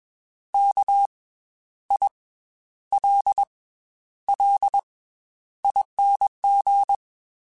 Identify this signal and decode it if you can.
Morse code, heard as a keyed tone, about 21 wpm, 800 Hz. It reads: K I L L ING
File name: Special INGs / Special INGs-mp3s / Slow ing @20